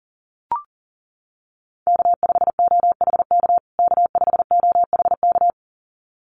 Morse code, heard as a keyed tone, about 37 wpm, 700 Hz. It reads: K5OHK K5OHK
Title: USA Call Signs Repeated 2x 40wpm